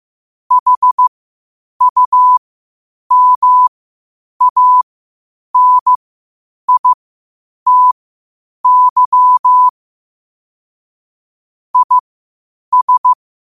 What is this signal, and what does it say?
Morse code, heard as a keyed tone, about 15 wpm, spaced as for 8 wpm, 1000 Hz. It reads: HUMANITY IS
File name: qotdfiles / 8wpm